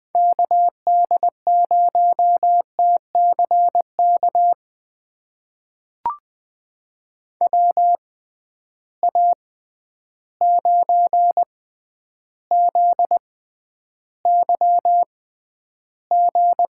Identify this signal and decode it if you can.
Morse code, heard as a keyed tone, about 20 wpm, spaced as for 10 wpm, 700 Hz. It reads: KD0TCK WA9ZYG
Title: USA Call Signs - 6 Characters - 6x Character Spacing - 20wpm